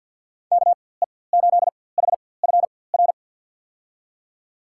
Morse code, heard as a keyed tone, about 52 wpm, spaced as for 23 wpm, 700 Hz. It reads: KE8HFR
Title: USA Call Signs - 6 Characters - 4x Character Spacing - 50wpm